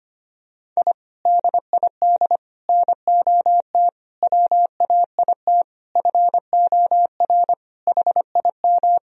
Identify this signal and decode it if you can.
Morse code, heard as a keyed tone, about 25 wpm, 700 Hz. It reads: I DID NOT WAIT FOR HIM
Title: Sentences from Top 1000 Words - Review Part 2 - 25wpm